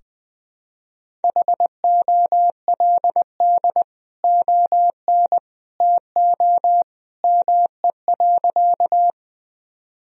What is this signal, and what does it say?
Morse code, heard as a keyed tone, about 20 wpm, 700 Hz. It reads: HOLD ON TO ME.
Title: Sentences from Top 1000 Words - Review Part 2 - 20wpm